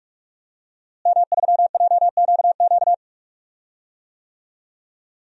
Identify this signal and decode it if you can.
Morse code, heard as a keyed tone, about 46 wpm, 700 Hz. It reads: M3JQQ
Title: DXCC-100 Call Signs - 5 Characters - Repeated 3x - 45wpm